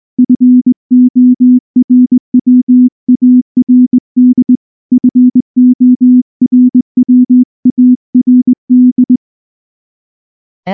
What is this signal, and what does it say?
Morse code, heard as a keyed tone, about 20 wpm, 250 Hz. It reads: FORWARD FORWARD
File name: morse / morse-training-20WPM-1